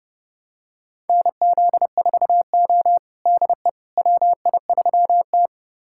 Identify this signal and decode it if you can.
Morse code, heard as a keyed tone, about 30 wpm, 700 Hz. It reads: NZ4O DE WI3T
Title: USA Callsign DE Callsign - 4 Characters - 30wpm